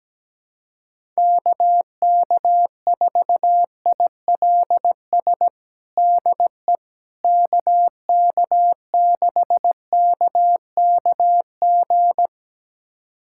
Morse code, heard as a keyed tone, about 17 wpm, 700 Hz. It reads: KK4ILS DE KK6KKG
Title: USA Callsign DE Callsign - 6 Characters - 17wpm